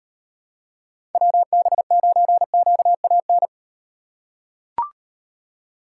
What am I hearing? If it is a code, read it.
Morse code, heard as a keyed tone, about 38 wpm, 700 Hz. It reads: WB9QAN